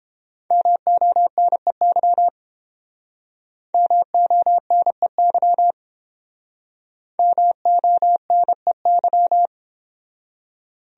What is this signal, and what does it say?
Morse code, heard as a keyed tone, about 29 wpm, 700 Hz. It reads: MONEY MONEY MONEY